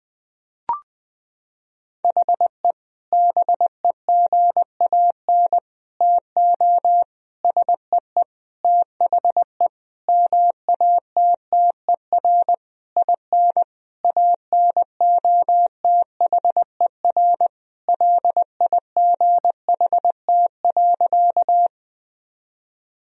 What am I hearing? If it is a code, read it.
Morse code, heard as a keyed tone, about 20 wpm, 700 Hz. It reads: HE BEGAN TO SEE THE MATTER IN ANOTHER LIGHT.
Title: Sentences from Top 1000 Words - Review Part 1 - 20wpm